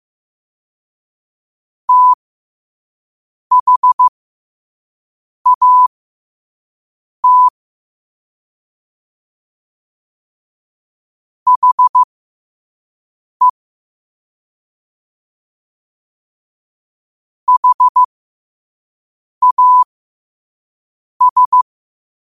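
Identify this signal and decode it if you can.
Morse code, heard as a keyed tone, about 15 wpm, spaced as for 5 wpm, 1000 Hz. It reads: THAT HE HAS
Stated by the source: Morse code practice at 5 wpm.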